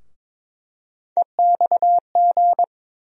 Morse code, heard as a keyed tone, about 22 wpm, 700 Hz. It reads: EXG
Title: USA Call Sign Postfixes - 3 Letters 22wpm